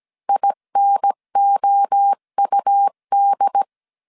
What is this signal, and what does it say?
Morse code, heard as a keyed tone, about 17 wpm, 800 Hz. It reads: INOUD
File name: CW / DET / 1 3 1500C S A 16wpm 2141